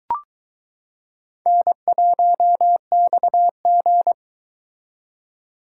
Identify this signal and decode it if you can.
Morse code, heard as a keyed tone, about 22 wpm, 700 Hz. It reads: N1XG